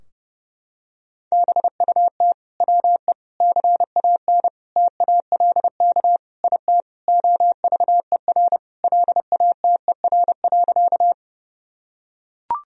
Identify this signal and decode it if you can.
Morse code, heard as a keyed tone, about 30 wpm, 700 Hz. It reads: BUT WE CAN TALK IT OVER LATER.
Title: Sentences from Top 1000 Words - Review Part 4 - 30wpm